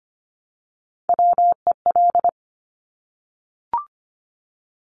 Morse code, heard as a keyed tone, about 25 wpm, 700 Hz. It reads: WEL
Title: Top Three Letter Combinations in Words 25wpm